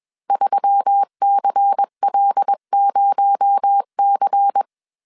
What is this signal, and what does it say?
Morse code, heard as a keyed tone, about 21 wpm, 800 Hz. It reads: 3CL0C
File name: CW / DET / 1 6 1500C S C 20wpm 2116